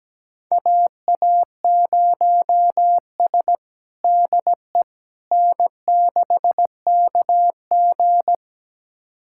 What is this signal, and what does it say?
Morse code, heard as a keyed tone, about 17 wpm, 700 Hz. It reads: AA0S DE N6KG
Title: USA Callsign DE Callsign - 4 Characters - 17wpm